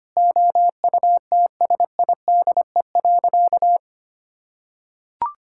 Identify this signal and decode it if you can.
Morse code, heard as a keyed tone, about 25 wpm, 700 Hz. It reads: OUTSIDE.
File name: practice / 25wpm / Sentences with 3-Word Phrases - 25wpm